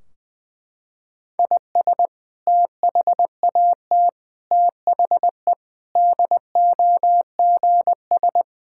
Morse code, heard as a keyed tone, about 20 wpm, 700 Hz. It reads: IS THAT THE DOGS
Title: Sentences from Top 1600 Words 20wpm